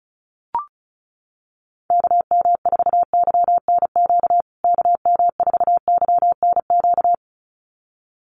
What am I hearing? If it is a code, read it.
Morse code, heard as a keyed tone, about 34 wpm, 700 Hz. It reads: KM4YNQ KM4YNQ
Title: USA Call Signs Repeated 2x 35wpm